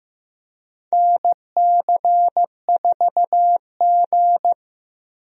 Morse code, heard as a keyed tone, about 15 wpm, 700 Hz. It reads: NC4G
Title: USA Call Signs - 4 Characters - Repeated 2x - 15wpm